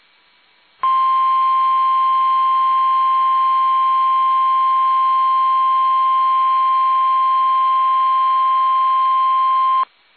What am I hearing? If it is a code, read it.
Emergency Alert System (SAME) tones (the 1050 Hz attention tone). It is heard from 0.8 to 9.9 s.